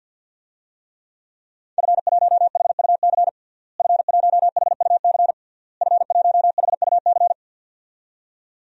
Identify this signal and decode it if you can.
Morse code, heard as a keyed tone, about 46 wpm, 700 Hz. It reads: F1HUC F1HUC F1HUC
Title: DXCC-100 Call Signs - 5 Characters - Repeated 3x - 50wpm